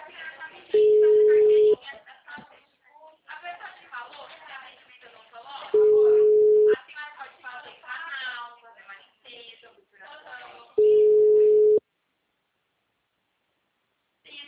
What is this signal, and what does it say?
Ringback tone (Europe, 425 Hz), with speech.
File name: out- -2022